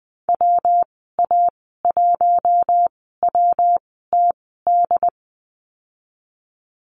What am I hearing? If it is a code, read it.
Morse code, heard as a keyed tone, about 19 wpm, spaced as for 14 wpm, 700 Hz. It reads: WA1WTD
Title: 147 Lesson 24g - Letter y - Call Signs 20wpm